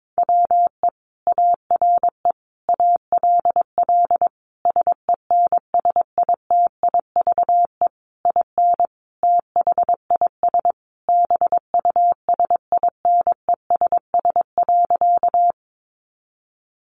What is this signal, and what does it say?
Morse code, heard as a keyed tone, about 22 wpm, 700 Hz. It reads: WE ARE ALL SENSITIVE IN THIS BUSINESS.
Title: Sentences from Top 1400 Words 22wpm